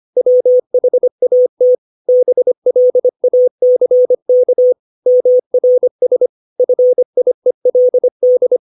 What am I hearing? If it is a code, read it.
Morse code, heard as a keyed tone, about 25 wpm, 500 Hz. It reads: WHAT BLACK MRS FIELD
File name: 632 words / 632 Words 25WPM